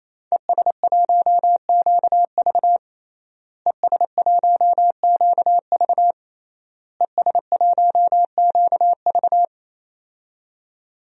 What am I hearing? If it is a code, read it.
Morse code, heard as a keyed tone, about 28 wpm, 700 Hz. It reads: ES1QV ES1QV ES1QV
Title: DXCC-100 Call Signs - 5 Characters - Repeated 3x - 28wpm